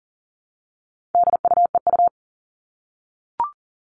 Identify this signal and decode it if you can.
Morse code, heard as a keyed tone, about 40 wpm, 700 Hz. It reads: DUEU
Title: All Cut Numbers - 4-Digit 40wpm